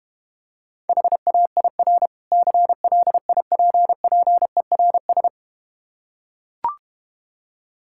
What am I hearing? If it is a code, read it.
Morse code, heard as a keyed tone, about 31 wpm, 700 Hz. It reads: HAIR CLIPPERS